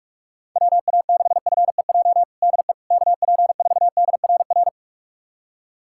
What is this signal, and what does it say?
Morse code, heard as a keyed tone, about 47 wpm, 700 Hz. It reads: WA6FEJ DE KP4DRR
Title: USA Callsign DE Callsign - 6 Characters - 45wpm